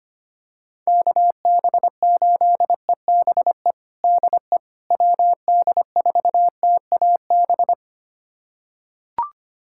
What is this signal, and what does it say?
Morse code, heard as a keyed tone, about 25 wpm, 700 Hz. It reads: KB8EBE DE WD4TAB
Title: USA Callsign DE Callsign - 6 Characters - 25wpm